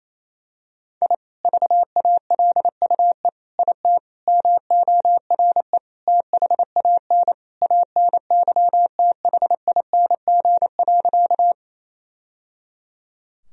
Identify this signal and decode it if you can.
Morse code, heard as a keyed tone, about 27 wpm, 700 Hz. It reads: I VALUE IT MORE THAN ANYTHING.
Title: Sentences from Top 1000 Words - Review Part 3 - 28wpm